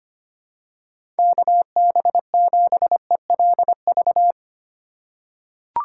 Morse code, heard as a keyed tone, about 25 wpm, 700 Hz. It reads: KB7ELV
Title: USA Call Signs - 6 Characters - 3x Character Spacing - 25wpm